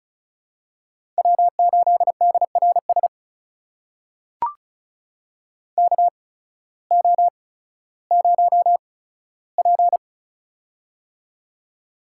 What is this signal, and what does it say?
Morse code, heard as a keyed tone, about 36 wpm, spaced as for 16 wpm, 700 Hz. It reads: W8DRS KO0P
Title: USA Call Signs - 8x Character Spacing - 35wpm